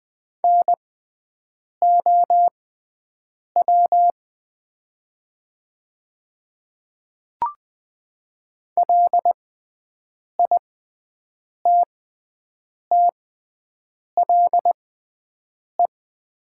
Morse code, heard as a keyed tone, about 20 wpm, spaced as for 7 wpm, 700 Hz. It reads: NOW LITTLE
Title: 6x Character Spacing - Top 200 Words - 20wpm